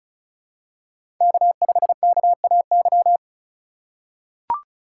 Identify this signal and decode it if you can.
Morse code, heard as a keyed tone, about 34 wpm, 700 Hz. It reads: K5KAY
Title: USA Call Signs - 5 Characters - 4x Character Spacing - 35wpm